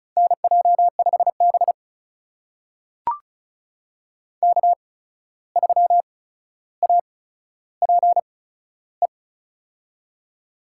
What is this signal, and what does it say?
Morse code, heard as a keyed tone, about 36 wpm, spaced as for 14 wpm, 700 Hz. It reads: NJ5B K3APE
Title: USA Call Signs - 8x Character Spacing - 35wpm